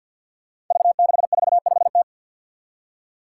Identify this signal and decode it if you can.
Morse code, heard as a keyed tone, about 53 wpm, 700 Hz. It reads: V645T
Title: Five Challenging Letter-Number Combinations 50wpm